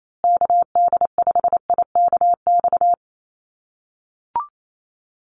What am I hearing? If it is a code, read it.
Morse code, heard as a keyed tone, about 28 wpm, 700 Hz. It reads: KD5IKX
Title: USA Call Signs Repeated 2x 28wpm